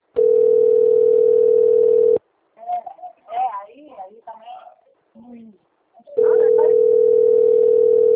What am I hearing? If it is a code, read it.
Ringback tone (North America), 440 + 480 Hz, with speech.
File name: rg-1600-unknown- - - .40159